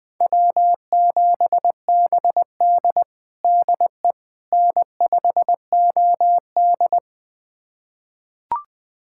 Morse code, heard as a keyed tone, about 20 wpm, 700 Hz. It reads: W7BD DE N5OD
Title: USA Callsign DE Callsign - 4 Characters - 20wpm